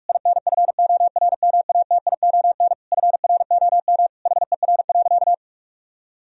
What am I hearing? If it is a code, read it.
Morse code, heard as a keyed tone, about 46 wpm, 700 Hz. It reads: INFORMATION FROM HER.
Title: Sentences from Top 1000 Words - Review Part 3 - 45wpm